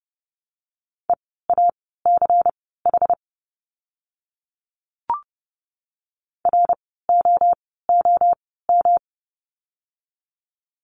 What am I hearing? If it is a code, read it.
Morse code, heard as a keyed tone, about 30 wpm, spaced as for 17 wpm, 700 Hz. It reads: EACH ROOM
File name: course / 30:10wpm / 115 Lesson 19f - Letter w - Words 30wpm